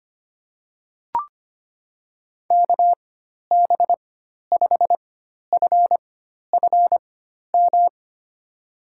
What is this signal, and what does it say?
Morse code, heard as a keyed tone, about 25 wpm, spaced as for 12 wpm, 700 Hz. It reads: KB5FFM
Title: USA Call Signs - 4x Character Spacing - 25wpm